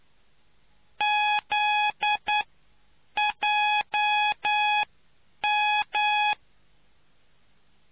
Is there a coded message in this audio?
Yes: Morse code from 1.0 to 6.3 s, heard as a keyed tone, about 9 wpm, spaced as for 7 wpm, 840 Hz. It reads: ZJM